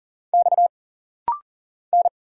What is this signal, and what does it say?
Morse code, heard as a keyed tone, about 40 wpm, 700 Hz. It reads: X N
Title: Single Letters - Rapid-Fire 40wpm